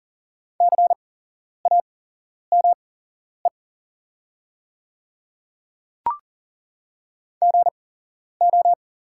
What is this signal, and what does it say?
Morse code, heard as a keyed tone, about 40 wpm, spaced as for 11 wpm, 700 Hz. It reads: CAME GO
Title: 8x Character Spacing - Top 200 Words - 40wpm